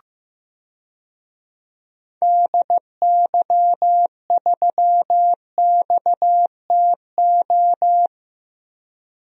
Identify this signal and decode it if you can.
Morse code, heard as a keyed tone, about 15 wpm, 700 Hz. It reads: DY3XTO
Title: DXCC-100 Call Signs - 6 Characters - Repeated 3x - 15wpm